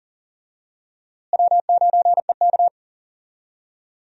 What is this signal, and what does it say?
Morse code, heard as a keyed tone, about 37 wpm, 700 Hz. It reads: W9EK
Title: USA Call Signs - 4 Characters - Repeated 2x - 40wpm